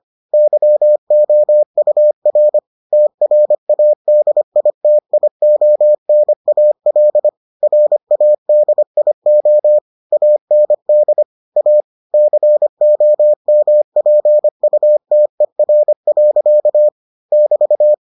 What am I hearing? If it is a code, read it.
Morse code, heard as a keyed tone, about 25 wpm, 600 Hz. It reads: YOUR TRADITIONAL RADIO AND A COMPUTER. <BT>